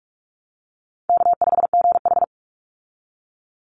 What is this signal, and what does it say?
Morse code, heard as a keyed tone, about 44 wpm, 700 Hz. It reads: K5GH
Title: Easy Call Signs Repeated 2x - 45wpm